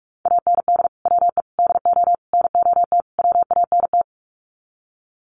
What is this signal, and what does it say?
Morse code, heard as a keyed tone, about 44 wpm, 700 Hz. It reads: AND WE DO NOT WANT
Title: Sets of 5 Words Encore2 45wpm